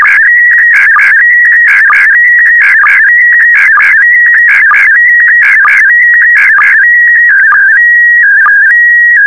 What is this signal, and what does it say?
SSTV (slow-scan television), PD 290.
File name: sstv / PD290-VK4MSL